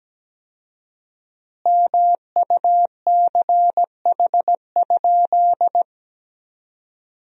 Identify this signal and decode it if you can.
Morse code, heard as a keyed tone, about 17 wpm, 700 Hz. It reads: MUCH?
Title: 4x Word-Spacing - Sentences from Top 200 Words - 17wpm